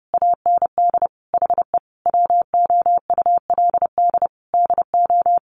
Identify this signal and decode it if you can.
Morse code, heard as a keyed tone, about 30 wpm, 700 Hz. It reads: AND HE WOULD DO
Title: Sentences from Top 1500 Words 30wpm